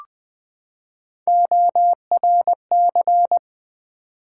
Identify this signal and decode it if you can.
Morse code, heard as a keyed tone, about 20 wpm, 700 Hz. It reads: ORC
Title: Build A Word - 3 Letters At A Time - 9-Letter Words - 20wpm